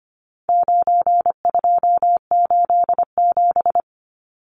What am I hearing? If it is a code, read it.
Morse code, heard as a keyed tone, about 25 wpm, 700 Hz. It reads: 9287
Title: Numbers - 4-Digit 25wpm